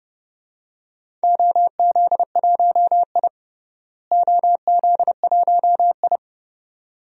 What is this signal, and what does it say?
Morse code, heard as a keyed tone, about 30 wpm, 700 Hz. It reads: OZ1I OZ1I
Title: DXCC-100 Call Signs - 4 Characters - Repeated 2x - 30wpm